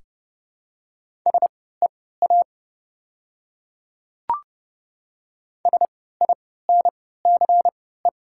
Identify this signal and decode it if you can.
Morse code, heard as a keyed tone, about 31 wpm, spaced as for 17 wpm, 700 Hz. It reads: SEA SINCE
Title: 3x Character Spacing - Top 300 Words - 30wpm